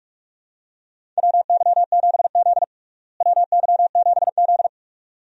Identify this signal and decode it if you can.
Morse code, heard as a keyed tone, about 46 wpm, 700 Hz. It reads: WY7Z WY7Z
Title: USA Call Signs - 4 Characters - Repeated 2x - 45wpm